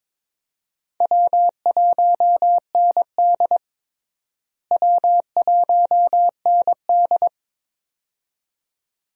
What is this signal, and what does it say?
Morse code, heard as a keyed tone, about 22 wpm, 700 Hz. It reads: W1ND W1ND
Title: USA Call Signs - 4 Characters - Repeated 2x - 22wpm